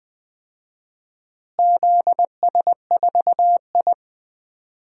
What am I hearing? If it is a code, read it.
Morse code, heard as a keyed tone, about 21 wpm, 700 Hz. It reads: ZS4I
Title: DXCC-100 Call Signs - 4 Characters - Repeated 2x - 20wpm